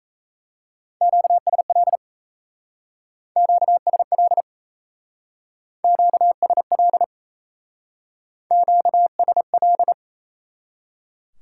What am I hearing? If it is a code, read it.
Morse code, heard as a keyed tone, about 36 wpm, 700 Hz. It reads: QSL QSL QSL QSL